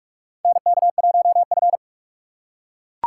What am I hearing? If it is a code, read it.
Morse code, heard as a keyed tone, about 41 wpm, 700 Hz. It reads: NK1F
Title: USA Call Signs - 4 Characters - 6x Character Spacing - 45wpm